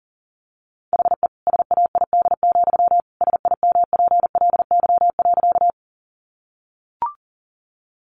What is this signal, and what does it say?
Morse code, heard as a keyed tone, about 40 wpm, 700 Hz. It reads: HE SAID, SIMPLY.
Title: Sentences from Top 600 Words 40wpm